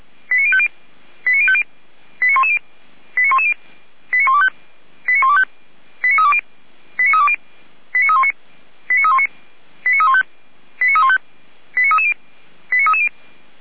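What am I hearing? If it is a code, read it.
Five-tone selective call (ZVEI), heard as quick runs of five tones, tones of about 70 ms. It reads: 89050 89050 89110 89110 89125 89125 89320 89320 89319 89319 89315 89315 89330 89330
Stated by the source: ZVEI five-tone selective call 89187, received via SDR.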